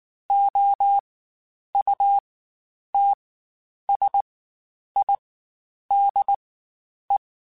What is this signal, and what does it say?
Morse code, heard as a keyed tone, about 19 wpm, spaced as for 9 wpm, 800 Hz. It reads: OUTSIDE
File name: Spaced Characters / Spaced Characters-mp3s / Weather5 22 sp @18